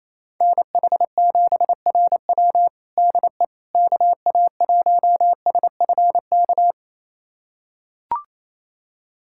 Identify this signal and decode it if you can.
Morse code, heard as a keyed tone, about 27 wpm, 700 Hz. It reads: NH7RW DE KA1SFK
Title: USA Callsign DE Callsign 28wpm